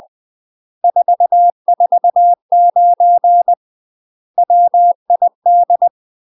Morse code, heard as a keyed tone, about 20 wpm, 700 Hz. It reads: 449 WID
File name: UK MASTER QSOs TRAINING / 20WPM / UK MASTER QSOs 0024